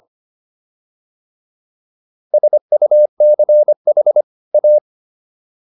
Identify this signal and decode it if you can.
Morse code, heard as a keyed tone, about 25 wpm, 600 Hz. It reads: SUCH A